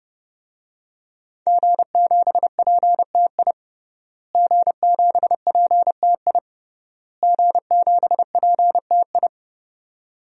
Morse code, heard as a keyed tone, about 30 wpm, 700 Hz. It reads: G7PTI G7PTI G7PTI
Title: DXCC-100 Call Signs - 5 Characters - Repeated 3x - 30wpm